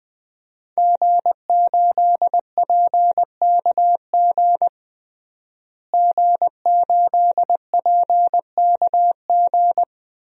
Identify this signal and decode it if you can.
Morse code, heard as a keyed tone, about 20 wpm, 700 Hz. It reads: G8PKG G8PKG
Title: DXCC-100 Call Signs - 5 Characters - Repeated 3x - 20wpm